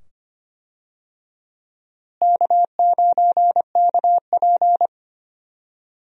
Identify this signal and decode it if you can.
Morse code, heard as a keyed tone, about 25 wpm, 700 Hz. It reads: K9KP
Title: USA Call Signs - 4 Characters - Repeated 2x - 25wpm